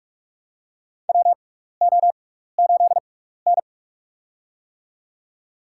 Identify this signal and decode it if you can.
Morse code, heard as a keyed tone, about 46 wpm, spaced as for 16 wpm, 700 Hz. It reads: WO8N
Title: USA Call Signs - 6x Character Spacing - 45wpm